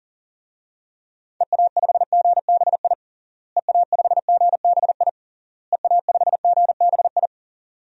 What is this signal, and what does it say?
Morse code, heard as a keyed tone, about 40 wpm, 700 Hz. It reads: EA5GBI EA5GBI EA5GBI
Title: DXCC-100 Call Signs - 6 Characters - Repeated 3x - 40wpm